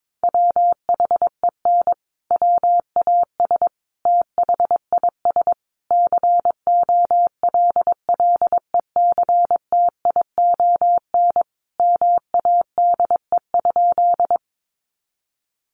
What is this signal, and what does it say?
Morse code, heard as a keyed tone, about 22 wpm, 700 Hz. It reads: WHEN WAS THIS COLLECTION MADE?
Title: Sentences from Top 1200 Words 22wpm